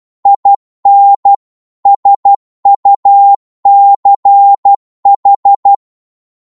Morse code, heard as a keyed tone, about 12 wpm, 800 Hz. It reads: IN SUCH
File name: 12wpm